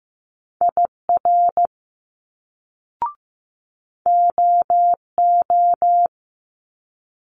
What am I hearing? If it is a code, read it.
Morse code, heard as a keyed tone, about 15 wpm, 700 Hz. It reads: IR OO
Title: Multi-letter Phonemes Quick Fire 15wpm